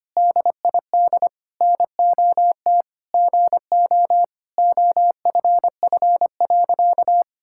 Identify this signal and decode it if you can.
Morse code, heard as a keyed tone, about 25 wpm, 700 Hz. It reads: DID NOT GO OFF.
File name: practice / 25wpm / Sentences with 4-Word Phrases - 25wpm